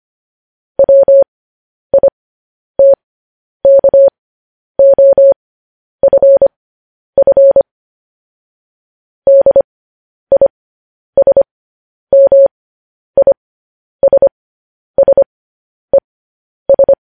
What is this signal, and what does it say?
Morse code, heard as a keyed tone, about 25 wpm, spaced as for 10 wpm, 550 Hz. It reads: WITKOFF DISMISSES
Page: news-headlines-in-morse-code-at-10-wpm